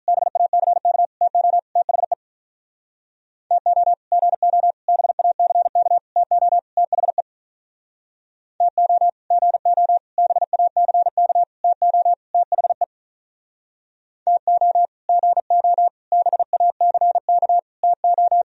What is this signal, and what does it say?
Morse code, heard as a keyed tone, about 44 wpm, 700 Hz. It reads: BACK TO THE TO GO BACK TO THE TO GO BACK TO THE TO GO BACK TO